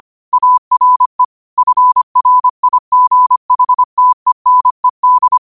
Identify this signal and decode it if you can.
Morse code, heard as a keyed tone, about 25 wpm, 1000 Hz. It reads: ARE FRIGHTENED